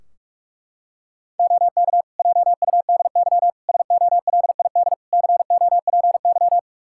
Morse code, heard as a keyed tone, about 46 wpm, 700 Hz. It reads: OK JUDY SOLID COPY
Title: Practice QSO - Exchange 3 - More Info - 45wpm